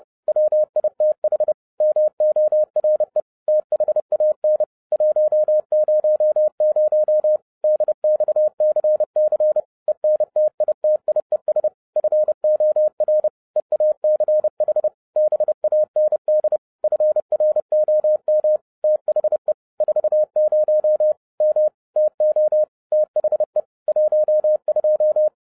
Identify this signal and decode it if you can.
Morse code, heard as a keyed tone, about 30 wpm, 610 Hz. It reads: WITH MORE THAN 100 DXCC ENTITIES FOR EACH BAND FROM THE 40 M TO THE 12
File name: 2012-01-mp3 / n11-I DIDNT MISS THE SOLAR CYCLE 24-30wpm